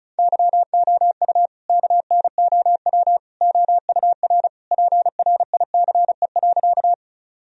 Morse code, heard as a keyed tone, about 36 wpm, 700 Hz. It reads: YOU KNOW OUR PRICE.
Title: Sentences from Top 1000 Words - Review Part 4 - 35wpm